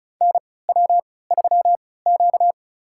Morse code, heard as a keyed tone, about 36 wpm, spaced as for 20 wpm, 700 Hz. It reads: NW3Q
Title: USA Call Signs - 4 Characters - 3x Character Spacing - 35wpm